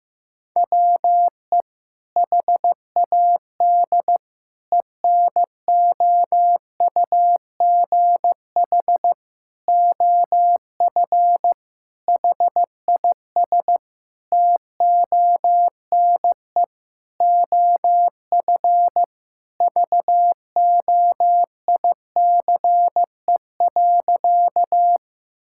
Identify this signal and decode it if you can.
Morse code, heard as a keyed tone, about 15 wpm, 700 Hz. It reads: WE HAD ENOUGH OF HIS TONE OF VOICE.
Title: Sentences from Top 1600 Words 15wpm